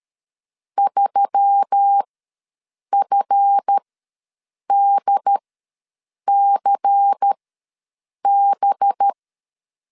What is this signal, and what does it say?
Morse code, heard as a keyed tone, about 13 wpm, spaced as for 7 wpm, 800 Hz. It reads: 3FDCB
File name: LEZIONI / 8 YJ45 / 8 YJ ws 12wpm 3342